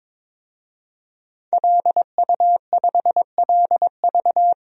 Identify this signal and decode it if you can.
Morse code, heard as a keyed tone, about 22 wpm, 700 Hz. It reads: LU5LV
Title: DXCC-100 Call Signs - 5 Characters - Repeated 3x - 22wpm